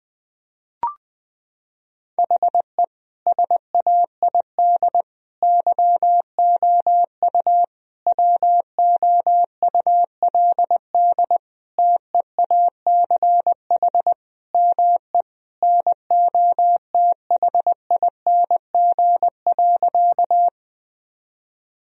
Morse code, heard as a keyed tone, about 20 wpm, 700 Hz. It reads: HE SAID YOU WOULD TEACH ME NOTHING.
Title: Sentences from Top 1000 Words - Review Part 2 - 20wpm